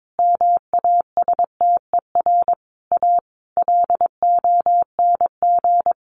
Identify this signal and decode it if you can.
Morse code, heard as a keyed tone, about 22 wpm, 700 Hz. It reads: MASTER A LONG
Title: Sentences from Top 1300 Words 22wpm